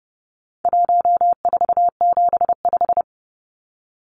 Morse code, heard as a keyed tone, about 30 wpm, 700 Hz. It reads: 1475
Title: Numbers - 4-Digit 30wpm